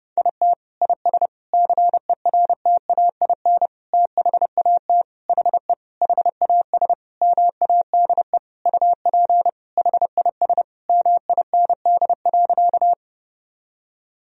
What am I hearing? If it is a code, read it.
Morse code, heard as a keyed tone, about 32 wpm, 700 Hz. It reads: IT IS CERTAIN THAT HE HAS MADE UP HIS MIND.
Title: Sentences from Top 1000 Words - Review Part 2 - 30wpm